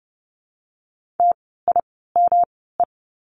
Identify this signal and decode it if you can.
Morse code, heard as a keyed tone, about 30 wpm, spaced as for 17 wpm, 700 Hz. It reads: TIME
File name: course / 30:10wpm / 237 Lesson 38f - Letter q - Words 30wpm